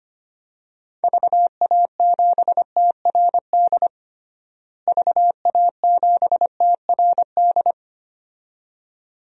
Morse code, heard as a keyed tone, about 25 wpm, 700 Hz. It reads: VA7TRD VA7TRD
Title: DXCC-100 Call Signs - 6 Characters - Repeated 2x - 25wpm